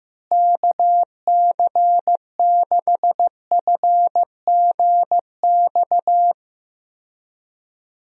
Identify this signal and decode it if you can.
Morse code, heard as a keyed tone, about 15 wpm, 700 Hz. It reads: KC6FGX
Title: USA Call Signs - 6 Characters - Repeated 3x - 15wpm